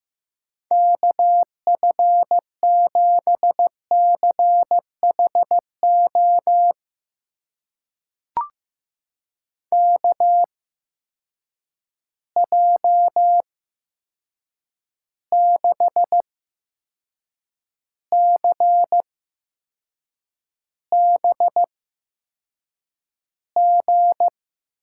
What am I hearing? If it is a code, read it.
Morse code, heard as a keyed tone, about 15 wpm, spaced as for 6 wpm, 700 Hz. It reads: KF7CHO KJ6CBG
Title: USA Call Signs - 6 Characters - 8x Character Spacing - 15wpm